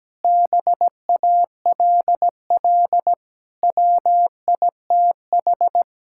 Morse code, heard as a keyed tone, about 17 wpm, 700 Hz. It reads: BALL WITH
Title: Sentences from Top 1000 Words - Review Part 6 - 17wpm